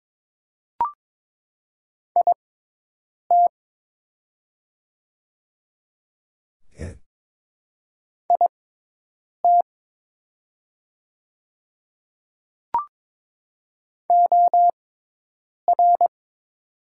Morse code, heard as a keyed tone, about 21 wpm, spaced as for 8 wpm, 700 Hz. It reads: IT IT OR
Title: 6x Character Spacing - Top 100 Words - 22wpm